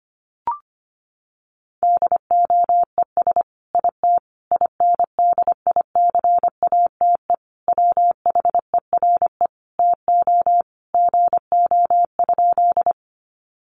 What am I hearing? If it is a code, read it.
Morse code, heard as a keyed tone, about 25 wpm, 700 Hz. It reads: DOES IT INDICATE WHERE TO GO?
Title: Sentences from Top 1300 Words 25wpm